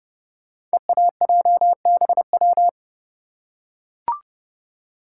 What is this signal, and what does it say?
Morse code, heard as a keyed tone, about 29 wpm, 700 Hz. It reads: EAJBW
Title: Code Groups - 5 Letters - 30wpm